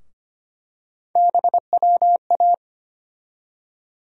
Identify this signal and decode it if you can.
Morse code, heard as a keyed tone, about 25 wpm, 700 Hz. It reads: BWA
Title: USA Call Sign Postfixes - 3 Letters 25wpm